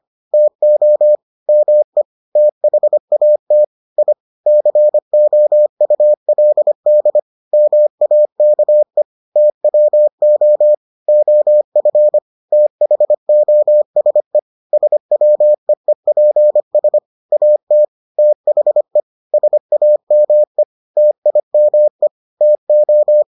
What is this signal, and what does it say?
Morse code, heard as a keyed tone, about 25 wpm, 600 Hz. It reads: TO ME THAT I COULD MAKE TWO OF THOSE SWEEPS AT THE SAME TIME TO